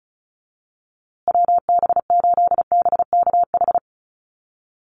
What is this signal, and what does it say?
Morse code, heard as a keyed tone, about 34 wpm, 700 Hz. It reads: WB8BKH
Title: USA Call Signs Repeated 3x 35wpm